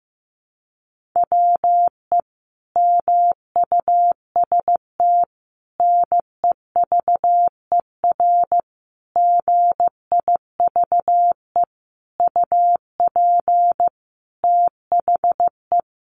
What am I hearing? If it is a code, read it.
Morse code, heard as a keyed tone, about 15 wpm, 700 Hz. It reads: WE MUST NEVER GIVE UP THE
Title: Sentences from Top 1500 Words 15wpm